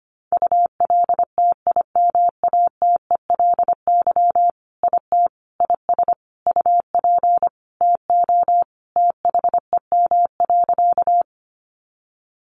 Morse code, heard as a keyed tone, about 25 wpm, 700 Hz. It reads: ULTIMATELY IT IS UP TO THEM.
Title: Sentences from Top 1200 Words 25wpm